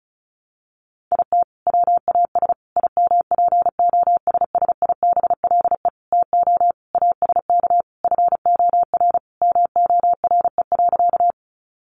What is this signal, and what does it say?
Morse code, heard as a keyed tone, about 34 wpm, 700 Hz. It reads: IT WAS IMPOSSIBLE TO ASK FOR MORE.
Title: Sentences from Top 1100 Words 35wpm